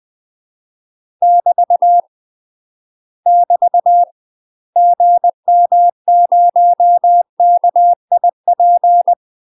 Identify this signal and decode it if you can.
Morse code, heard as a keyed tone, about 20 wpm, 700 Hz. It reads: <BT> <BT> GM0KIP
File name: UK MASTER QSOs TRAINING / 20WPM / UK MASTER QSOs 0074